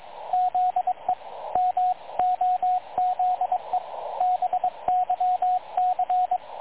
Morse code, heard as a keyed tone, about 22 wpm, 730 Hz, with radio noise. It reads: ZE MOZE BYC
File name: dlazaawansowanych / nagraniazpasma / websdr recording 3567.0kHz